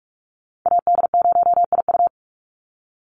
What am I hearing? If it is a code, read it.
Morse code, heard as a keyed tone, about 44 wpm, 700 Hz. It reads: AD0IU
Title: Call Signs - 45wpm